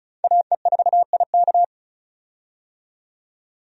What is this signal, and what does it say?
Morse code, heard as a keyed tone, about 34 wpm, 700 Hz. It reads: AE4IK
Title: USA Call Signs - 5 Characters - 35wpm